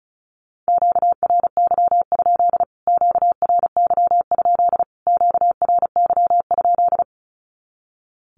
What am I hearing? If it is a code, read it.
Morse code, heard as a keyed tone, about 34 wpm, 700 Hz. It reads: QRY? QRY? QRY?
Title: Q-Codes Repeated 3x 35wpm